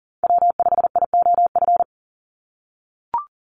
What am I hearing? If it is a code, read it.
Morse code, heard as a keyed tone, about 38 wpm, 700 Hz. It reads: W5IOF
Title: USA Call Signs 40wpm